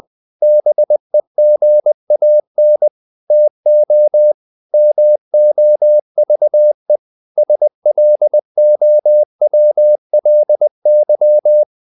Morse code, heard as a keyed tone, about 20 wpm, 600 Hz. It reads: BEGAN TO MOVE SLOWLY
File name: the old man and the sea / the old man and the sea part 1 20wpm